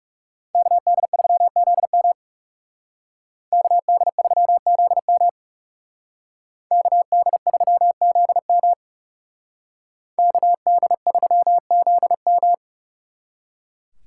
Morse code, heard as a keyed tone, about 38 wpm, 700 Hz. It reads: KD3ZM KD3ZM KD3ZM KD3ZM